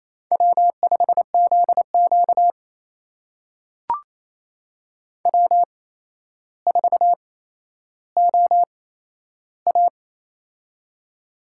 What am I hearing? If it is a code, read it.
Morse code, heard as a keyed tone, about 28 wpm, spaced as for 12 wpm, 700 Hz. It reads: W5ZQ W4OA
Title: USA Call Signs - 4 Characters - 8x Character Spacing - 28wpm